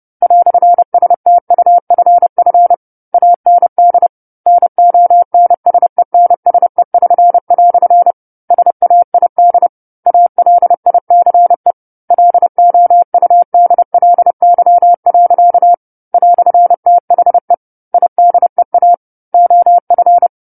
Morse code, heard as a keyed tone, about 30 wpm, 700 Hz. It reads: "STUFF AND NONSENSE<SN>" SAID ALICE LOUDLY. "THE IDEA OF
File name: 10-Chapter-10 30 WPM